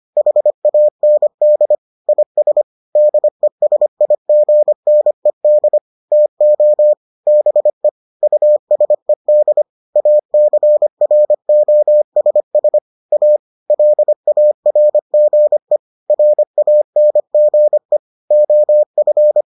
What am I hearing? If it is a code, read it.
Morse code, heard as a keyed tone, about 25 wpm, 600 Hz. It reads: HAND IS DESIGNED TO BE USED ACROSS A LARGE RANGE OF